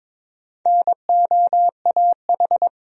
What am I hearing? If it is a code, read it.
Morse code, heard as a keyed tone, about 22 wpm, 700 Hz. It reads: NOAH
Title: USA Common Names - Max 4 Letters - 22wpm